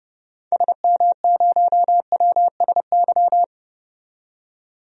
Morse code, heard as a keyed tone, about 30 wpm, 700 Hz. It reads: SM0WSY
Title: DXCC-100 Call Signs - 6 Characters - 30wpm